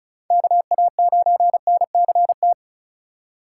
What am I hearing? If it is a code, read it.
Morse code, heard as a keyed tone, about 34 wpm, 700 Hz. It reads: KA9NCT
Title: USA Call Signs - 2x Character Spacing - 35wpm